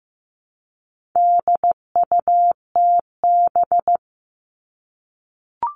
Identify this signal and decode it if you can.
Morse code, heard as a keyed tone, about 15 wpm, 700 Hz. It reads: DUTB
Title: All Cut Numbers - 4-Digit 15wpm